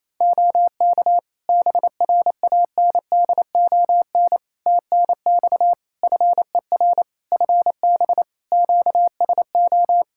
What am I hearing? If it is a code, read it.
Morse code, heard as a keyed tone, about 28 wpm, 700 Hz. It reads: OK BRANDON TNX FER FB QSO
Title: K1USN SST QSO Practice 28wpm